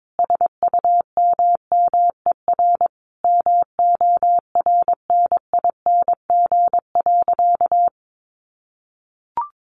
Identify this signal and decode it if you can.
Morse code, heard as a keyed tone, about 22 wpm, 700 Hz. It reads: SUMMER MORNING.
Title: Sentences from Top 1100 Words 22wpm